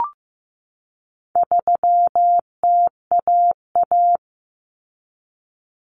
Morse code, heard as a keyed tone, about 15 wpm, 700 Hz. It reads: 3TAA
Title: Common Cut Numbers - 4-Digit 15wpm